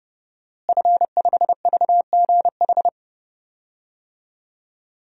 Morse code, heard as a keyed tone, about 30 wpm, 700 Hz. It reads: F5VGH
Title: DXCC-100 Call Signs - 5 Characters - Repeated 3x - 30wpm